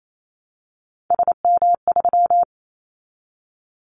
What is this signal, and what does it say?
Morse code, heard as a keyed tone, about 28 wpm, 700 Hz. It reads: SM3
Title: Callsign Prefixes for Top 25 DXCC - Repeated 3x - 28wpm